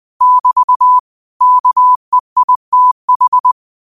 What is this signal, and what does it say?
Morse code, heard as a keyed tone, about 20 wpm, 1000 Hz. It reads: <BT> KEITH